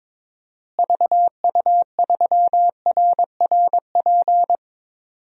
Morse code, heard as a keyed tone, about 22 wpm, 700 Hz. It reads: VU3RRP
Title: DXCC-100 Call Signs - 6 Characters - Repeated 2x - 22wpm